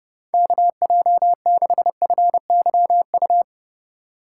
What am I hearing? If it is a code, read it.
Morse code, heard as a keyed tone, about 31 wpm, 700 Hz. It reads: KJ6FYU